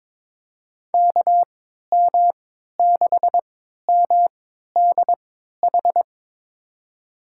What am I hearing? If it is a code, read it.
Morse code, heard as a keyed tone, about 21 wpm, spaced as for 12 wpm, 700 Hz. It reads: KM6MDH
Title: USA Call Signs - 6 Characters - 3x Character Spacing - 22wpm